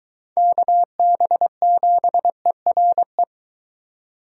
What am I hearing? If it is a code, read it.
Morse code, heard as a keyed tone, about 23 wpm, 700 Hz. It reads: KB7ERE